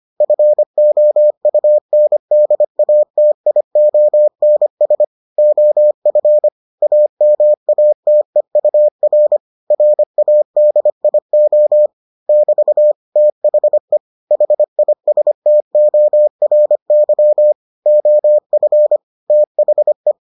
Morse code, heard as a keyed tone, about 25 wpm, 600 Hz. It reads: FOUNDATIONS OF AMATEUR RADIO <BT> THE HISTORY OF THE